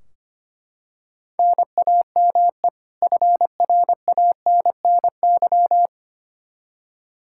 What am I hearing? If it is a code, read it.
Morse code, heard as a keyed tone, about 25 wpm, 700 Hz. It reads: NAME FRANNY
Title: Practice QSO - Exchange 1 - Beginning - 25wpm